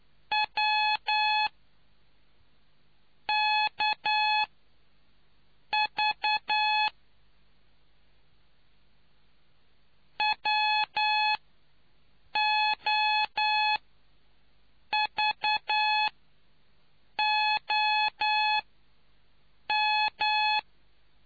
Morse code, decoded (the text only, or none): W KV WOVOM